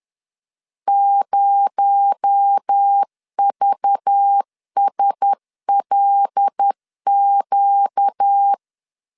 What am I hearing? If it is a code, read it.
Morse code, heard as a keyed tone, about 10 wpm, 800 Hz. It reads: 0VSLQ
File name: LEZIONI / 6 PKX0 / 6 PKX0 10wpm 2707